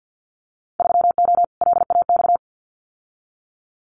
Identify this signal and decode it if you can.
Morse code, heard as a keyed tone, about 50 wpm, 700 Hz. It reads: 3O LAX
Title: Winter Field Day Exchanges Encore 50wpm